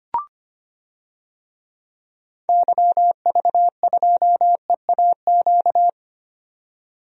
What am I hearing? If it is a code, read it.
Morse code, heard as a keyed tone, about 25 wpm, 700 Hz. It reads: YV2EAQ
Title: DXCC-100 Call Signs - 6 Characters - 25wpm